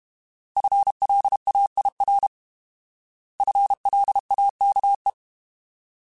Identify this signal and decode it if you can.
Morse code, heard as a keyed tone, about 32 wpm, 800 Hz. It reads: FLAIR FLAKE
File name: Alphabetical / Alphabetical mp3s / Fs @30